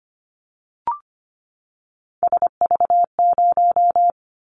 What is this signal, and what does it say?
Morse code, heard as a keyed tone, about 25 wpm, 700 Hz. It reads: SV0
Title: Callsign Prefixes for Top 50 DXCC - 25wpm